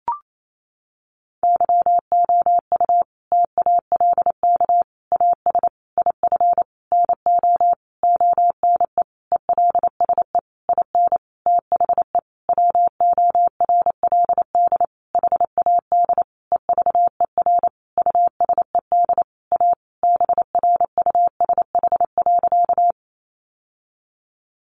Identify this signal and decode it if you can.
Morse code, heard as a keyed tone, about 28 wpm, 700 Hz. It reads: YOU TALK AS IF NO ONE ELSE IN THE WORLD HAD EVER USED A BRUSH.
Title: Sentences from Top 1100 Words 28wpm